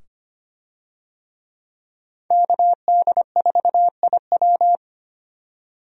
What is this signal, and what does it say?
Morse code, heard as a keyed tone, about 25 wpm, 700 Hz. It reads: KD4IW
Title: USA Call Signs - 5 Characters - 25wpm